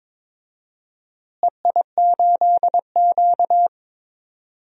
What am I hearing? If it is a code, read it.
Morse code, heard as a keyed tone, about 22 wpm, 700 Hz. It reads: EI8Q
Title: DXCC-100 Call Signs - 4 Characters - Repeated 2x - 22wpm